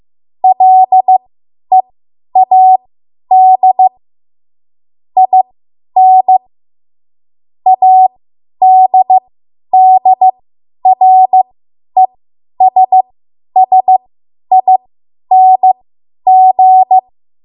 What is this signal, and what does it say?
Morse code, heard as a keyed tone, about 15 wpm, spaced as for 10 wpm, 750 Hz. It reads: LEAD IN ADDRESSING
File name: Morse / 10WPM